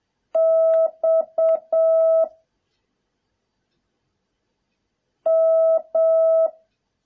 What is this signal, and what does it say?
Morse code, heard as a keyed tone, about 7 wpm, 650 Hz. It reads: X M